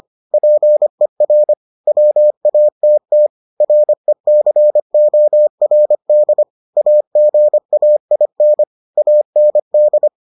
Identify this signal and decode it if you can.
Morse code, heard as a keyed tone, about 25 wpm, 600 Hz. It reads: PER WATT RECORD AGAIN AND